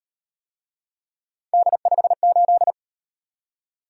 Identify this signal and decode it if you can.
Morse code, heard as a keyed tone, about 36 wpm, 700 Hz. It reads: D58